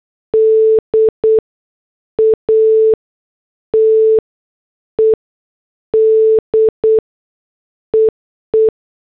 Morse code, heard as a keyed tone, about 8 wpm, spaced as for 6 wpm, 440 Hz. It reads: DATED EE
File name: morse / headlines